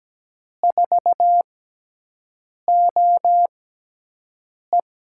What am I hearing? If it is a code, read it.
Morse code, heard as a keyed tone, about 17 wpm, spaced as for 6 wpm, 700 Hz. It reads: 4OE
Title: USA Call Signs - 5 Characters - 6x Character Spacing - 17wpm